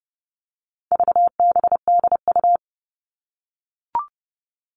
Morse code, heard as a keyed tone, about 30 wpm, 700 Hz. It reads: VBDU
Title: All Cut Numbers - 4-Digit 30wpm